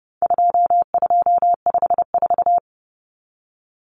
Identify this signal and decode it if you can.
Morse code, heard as a keyed tone, about 30 wpm, 700 Hz. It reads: 2254
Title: Common Cut Numbers - 4-Digit 30wpm